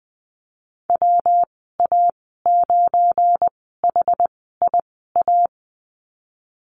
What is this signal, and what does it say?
Morse code, heard as a keyed tone, about 19 wpm, spaced as for 14 wpm, 700 Hz. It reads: WA9HIA
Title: 179 Lesson 29g - Slash - Call Signs 20wpm